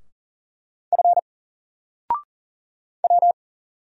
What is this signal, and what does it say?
Morse code, heard as a keyed tone, about 39 wpm, 700 Hz. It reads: F W
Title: Single Character (Letter:Number:Punctuation:Prosign) 40wpm